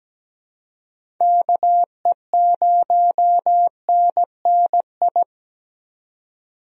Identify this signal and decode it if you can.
Morse code, heard as a keyed tone, about 17 wpm, 700 Hz. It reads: KE0NNI
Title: USA Call Signs - 6 Characters - 17wpm